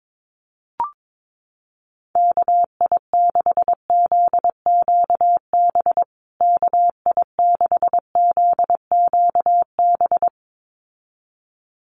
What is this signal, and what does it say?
Morse code, heard as a keyed tone, about 22 wpm, 700 Hz. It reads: KI6ZQB KI6ZQB
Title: USA Call Signs Repeated 2x 22wpm